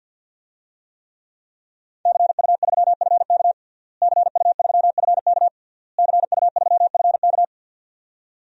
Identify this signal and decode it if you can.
Morse code, heard as a keyed tone, about 51 wpm, 700 Hz. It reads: CU3FK CU3FK CU3FK
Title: DXCC-100 Call Signs - 5 Characters - Repeated 3x - 50wpm